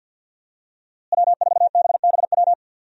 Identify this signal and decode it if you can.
Morse code, heard as a keyed tone, about 50 wpm, 700 Hz. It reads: W4BBW
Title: USA Call Signs - 5 Characters - Repeated 3x - 50wpm